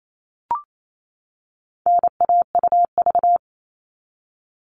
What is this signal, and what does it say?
Morse code, heard as a keyed tone, about 28 wpm, 700 Hz. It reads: NAUV
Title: All Cut Numbers - 4-Digit 28wpm